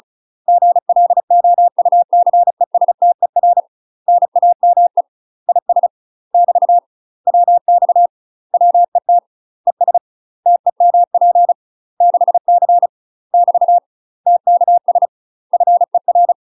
Morse code, heard as a keyed tone, about 36 wpm, 700 Hz. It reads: GLOUCESTER NAME IS <BT> WX WET ES TEMP 6C <BT> TKS FER
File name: UK MASTER QSOs TRAINING/35WPM